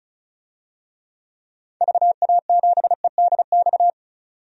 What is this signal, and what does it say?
Morse code, heard as a keyed tone, about 36 wpm, 700 Hz. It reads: VA7EDX
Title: DXCC-100 Call Signs - 6 Characters - 35wpm